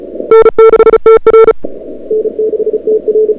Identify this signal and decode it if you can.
Morse code, heard as a keyed tone, about 38 wpm, 440 Hz. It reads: N6TR N6TR
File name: CWSprint / February 2007 CW Sprint / 01-2007 CW Sprint Hour1